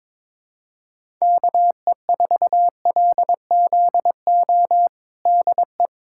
Morse code, heard as a keyed tone, about 22 wpm, 700 Hz. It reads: KE4LZO DE
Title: USA Callsign DE Callsign - 6 Characters - 22wpm